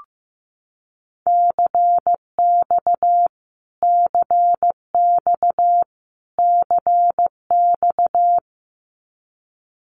Morse code, heard as a keyed tone, about 15 wpm, 700 Hz. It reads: CX CX CX
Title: Abbreviations Repeated 3x - 15wpm